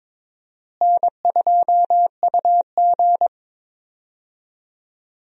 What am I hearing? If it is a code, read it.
Morse code, heard as a keyed tone, about 21 wpm, 700 Hz. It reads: N2UG
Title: USA Call Signs - 4 Characters - Repeated 3x - 22wpm